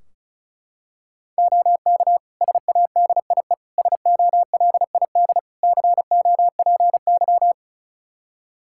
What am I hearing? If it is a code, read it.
Morse code, heard as a keyed tone, about 36 wpm, 700 Hz. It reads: OK SADIE SOLID COPY
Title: Practice QSO - Exchange 3 - More Info - 35wpm